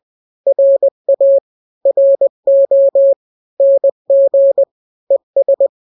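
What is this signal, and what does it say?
Morse code, heard as a keyed tone, about 20 wpm, 550 Hz. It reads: RA RO NG ES
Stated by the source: Morse drill of two-letter n-grams at 20/18 WPM (Farnsworth), from KK7RR.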